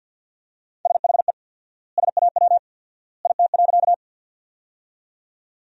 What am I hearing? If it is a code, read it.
Morse code, heard as a keyed tone, about 49 wpm, 700 Hz. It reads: SHE SAW IT.
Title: 4x Word-Spacing - Sentences from Top 300 Words - 50wpm